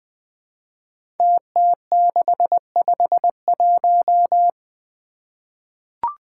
Morse code, heard as a keyed tone, about 20 wpm, 700 Hz. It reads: TT651
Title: Five Challenging Letter-Number Combinations 20wpm